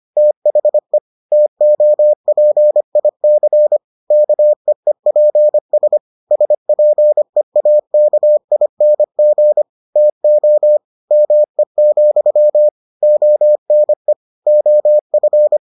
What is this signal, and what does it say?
Morse code, heard as a keyed tone, about 25 wpm, 600 Hz. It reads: THE TOPIC KEEPS SPEAKING TO ME, ONE OF